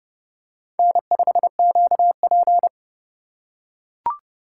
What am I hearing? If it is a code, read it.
Morse code, heard as a keyed tone, about 30 wpm, 700 Hz. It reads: N5QP